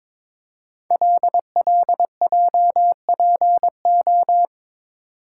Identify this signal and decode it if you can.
Morse code, heard as a keyed tone, about 22 wpm, 700 Hz. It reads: LLJPO
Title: Code Groups - 5 Letters - 22wpm